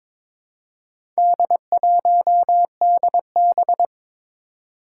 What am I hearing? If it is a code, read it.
Morse code, heard as a keyed tone, about 22 wpm, 700 Hz. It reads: D1DB
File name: practice / 22wpm / Four Letter-Number Combinations - 22wpm